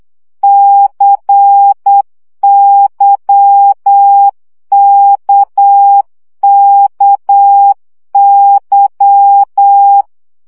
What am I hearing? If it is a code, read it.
Morse code, heard as a keyed tone, about 8 wpm, 800 Hz. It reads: CYKKY
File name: CW gyakorlas k c y 7 baud